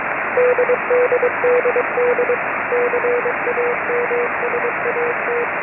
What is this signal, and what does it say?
Morse code, heard as a keyed tone, about 23 wpm, 530 Hz, with radio noise. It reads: DDDD CAMSAT
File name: cw beacon 145mhz972